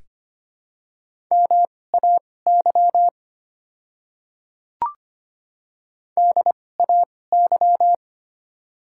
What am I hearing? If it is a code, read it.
Morse code, heard as a keyed tone, about 25 wpm, spaced as for 18 wpm, 700 Hz. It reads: MAY DAY
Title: 2x Character Spacing - Top 100 Words - 25wpm